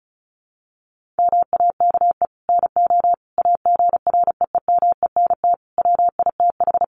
Morse code, heard as a keyed tone, about 34 wpm, 700 Hz. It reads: MAKE NO AGREEMENT WITH
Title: Sentences from Top 1500 Words 35wpm